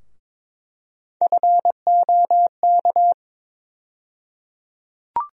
Morse code, heard as a keyed tone, about 22 wpm, 700 Hz. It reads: FOK
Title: USA Call Sign Postfixes - 3 Letters 22wpm